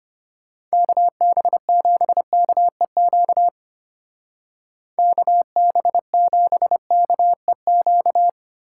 Morse code, heard as a keyed tone, about 32 wpm, 700 Hz. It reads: KB7KEQ KB7KEQ